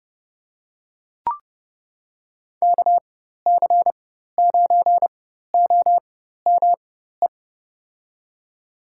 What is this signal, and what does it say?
Morse code, heard as a keyed tone, about 32 wpm, spaced as for 14 wpm, 700 Hz. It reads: KC9OME
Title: USA Call Signs - 6 Characters - 4x Character Spacing - 30wpm